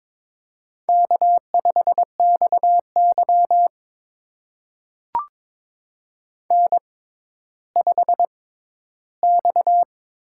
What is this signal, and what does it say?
Morse code, heard as a keyed tone, about 23 wpm, spaced as for 13 wpm, 700 Hz. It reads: K5XY N5X
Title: USA Call Signs - 4 Characters - 6x Character Spacing - 22wpm